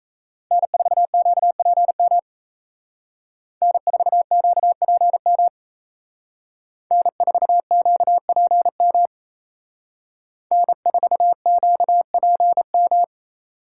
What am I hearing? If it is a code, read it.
Morse code, heard as a keyed tone, about 35 wpm, 700 Hz. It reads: N4QPM N4QPM N4QPM N4QPM
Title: Speed Racing - USA 1x3 Call Signs 28wpm